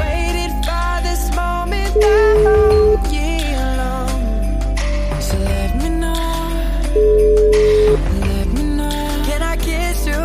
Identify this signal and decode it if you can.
Ringback tone (Europe, 425 Hz), with noise.